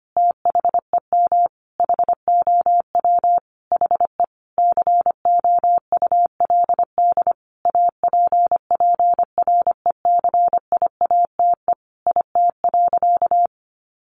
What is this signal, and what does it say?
Morse code, heard as a keyed tone, about 25 wpm, 700 Hz. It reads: THEM HOW HE COULD APPRECIATE IT.
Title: Sentences from Top 1300 Words 25wpm